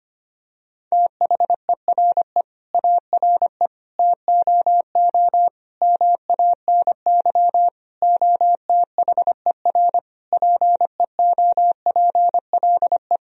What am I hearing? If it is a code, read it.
Morse code, heard as a keyed tone, about 25 wpm, 700 Hz. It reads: THERE ARE TOO MANY OTHER PEOPLE
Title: Sentences from Top 1000 Words - Review Part 3 - 25wpm